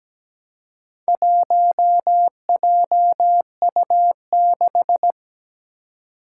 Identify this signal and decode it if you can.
Morse code, heard as a keyed tone, about 17 wpm, 700 Hz. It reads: 1JU6
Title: Four Challenging Letter-Number Combinations 17wpm